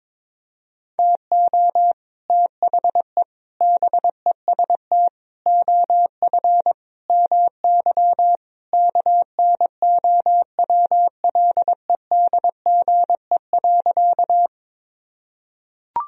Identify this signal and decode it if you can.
Morse code, heard as a keyed tone, about 22 wpm, 700 Hz. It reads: TO THE BEST OF MY KNOWLEDGE.
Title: Sentences from Top 1000 Words - Review Part 3 - 22wpm